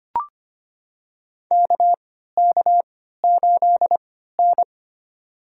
Morse code, heard as a keyed tone, about 25 wpm, spaced as for 14 wpm, 700 Hz. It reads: KK8N
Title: USA Call Signs - 3x Character Spacing - 25wpm